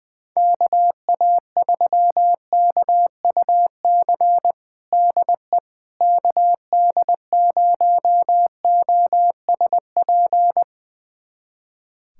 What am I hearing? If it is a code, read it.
Morse code, heard as a keyed tone, about 20 wpm, 700 Hz. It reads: KA3KUC DE KD0OSP
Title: USA Callsign DE Callsign 20wpm